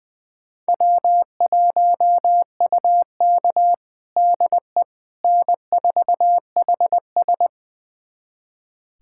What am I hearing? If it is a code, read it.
Morse code, heard as a keyed tone, about 20 wpm, 700 Hz. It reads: W1UK DE N4HS
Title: USA Callsign DE Callsign - 4 Characters - 20wpm